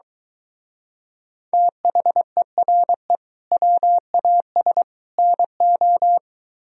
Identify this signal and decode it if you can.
Morse code, heard as a keyed tone, about 23 wpm, 700 Hz. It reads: THERE WAS NO